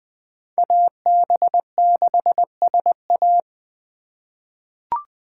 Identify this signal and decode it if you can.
Morse code, heard as a keyed tone, about 20 wpm, 700 Hz. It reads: AB6SA
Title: USA Call Signs - 5 Characters - 8x Character Spacing - 20wpm